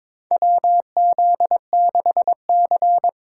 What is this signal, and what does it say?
Morse code, heard as a keyed tone, about 22 wpm, 700 Hz. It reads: WZ6C
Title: USA Call Signs - 4 Characters - 3x Character Spacing - 22wpm